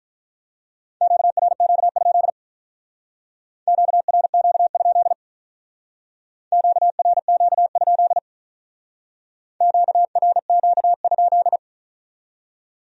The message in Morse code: QRQ? QRQ? QRQ? QRQ?

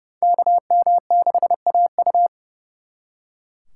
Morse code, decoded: KM6AU

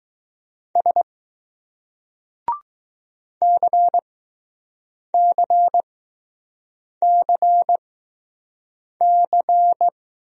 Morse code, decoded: S CCCC

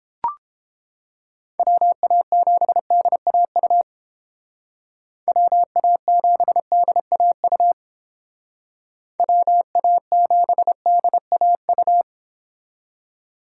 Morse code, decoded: WA7DAU WA7DAU WA7DAU

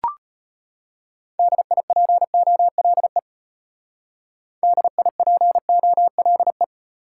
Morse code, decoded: DIPOLE DIPOLE

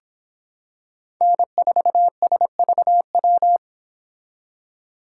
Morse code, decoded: N4SVW

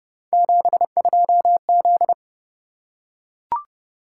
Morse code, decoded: 72Z

